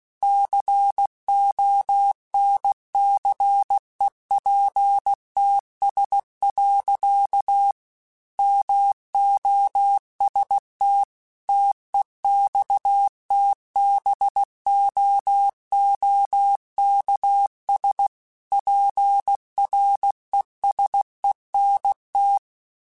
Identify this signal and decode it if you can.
Morse code, heard as a keyed tone, about 16 wpm, 800 Hz. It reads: CONCEPTS. MOST TEXTBOOKS PRESENT
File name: Technical Subjects / Technical Subjects-mp3s / Gates Intro @15